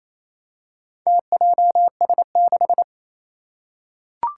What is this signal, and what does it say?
Morse code, heard as a keyed tone, about 28 wpm, 700 Hz. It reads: TJS6
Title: Four Challenging Letter-Number Combinations 28wpm